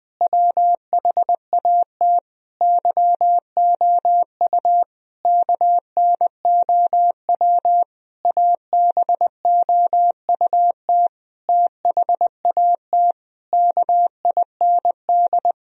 Morse code, heard as a keyed tone, about 20 wpm, 700 Hz. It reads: WHAT YOU KNOW ABOUT THAT KIND
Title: Sentences from Top 1000 Words - Review Part 4 - 20wpm